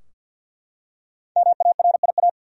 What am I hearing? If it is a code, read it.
Morse code, heard as a keyed tone, about 51 wpm, 700 Hz. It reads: MARIA
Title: USA Common Names - 50wpm